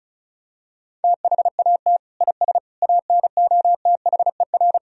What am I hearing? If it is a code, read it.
Morse code, heard as a keyed tone, about 34 wpm, 700 Hz. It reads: THAT IS ANOTHER
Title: Sentences from Top 1000 Words - Review Part 1 - 35wpm